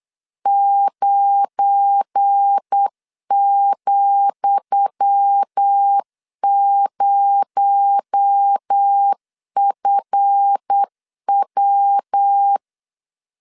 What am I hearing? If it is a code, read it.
Morse code, heard as a keyed tone, about 8 wpm, 800 Hz. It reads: 9,0FW
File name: LEZIONI / 10 SIGNS / 10D COMA 8wpm 3749